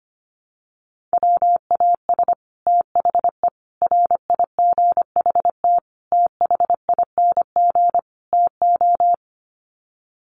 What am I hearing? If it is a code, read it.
Morse code, heard as a keyed tone, about 25 wpm, 700 Hz. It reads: WAS THE RIGHT THING TO
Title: Sets of 5 Words Encore2 25wpm